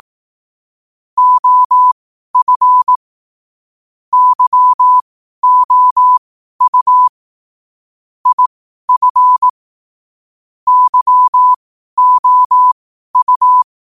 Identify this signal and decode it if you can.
Morse code, heard as a keyed tone, about 18 wpm, spaced as for 12 wpm, 1000 Hz. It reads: OF YOU IF YOU